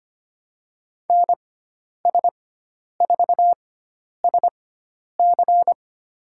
Morse code, heard as a keyed tone, about 25 wpm, spaced as for 10 wpm, 700 Hz. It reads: NS4SC
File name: Lesson 16g - Letter c - Call Signs 25wpm